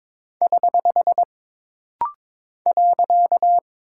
Morse code, heard as a keyed tone, about 22 wpm, 700 Hz. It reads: <HH> .
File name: practice / 22wpm / Single Punctuation-Prosign - Rapid-Fire - 22wpm